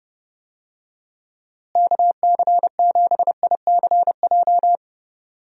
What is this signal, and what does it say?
Morse code, heard as a keyed tone, about 30 wpm, 700 Hz. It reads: KC7ICJ
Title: USA Call Signs - 6 Characters - Repeated 3x - 30wpm